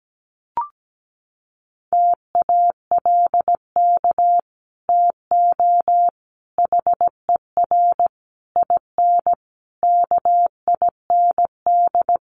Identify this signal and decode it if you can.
Morse code, heard as a keyed tone, about 17 wpm, 700 Hz. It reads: TALK TO HER IN KIND
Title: Sentences from Top 1500 Words 17wpm